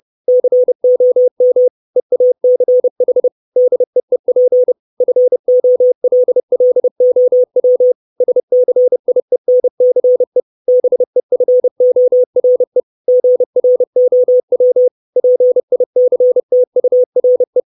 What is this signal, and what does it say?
Morse code, heard as a keyed tone, about 30 wpm, 500 Hz. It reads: COM EACH DEEP FOLLOW SCIENCE BEFORE GROW PICTURE
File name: 632 Words 30WPM